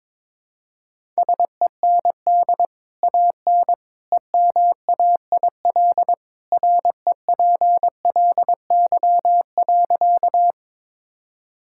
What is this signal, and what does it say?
Morse code, heard as a keyed tone, about 21 wpm, 700 Hz. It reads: SEND AN EMAIL REPLY.
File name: practice / 22wpm / Easy Sentences with Hints - 22wpm